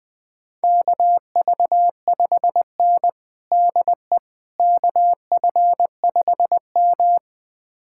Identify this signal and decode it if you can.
Morse code, heard as a keyed tone, about 21 wpm, 700 Hz. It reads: KV5N DE KF5M
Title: USA Callsign DE Callsign - 4 Characters - 20wpm